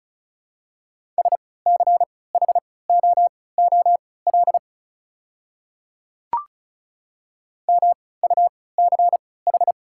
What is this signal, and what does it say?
Morse code, heard as a keyed tone, about 34 wpm, spaced as for 20 wpm, 690 Hz. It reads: SCHOOL MUCH
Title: 3x Character Spacing - Top 500 Words - 35wpm